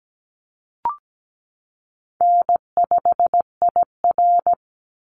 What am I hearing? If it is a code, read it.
Morse code, heard as a keyed tone, about 17 wpm, 700 Hz. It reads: N5IR
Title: Call Signs - 17wpm